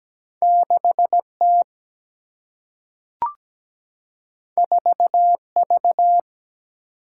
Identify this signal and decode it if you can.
Morse code, heard as a keyed tone, about 17 wpm, 700 Hz. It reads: 6T 4V
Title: Single-Letter Single-Number 17wpm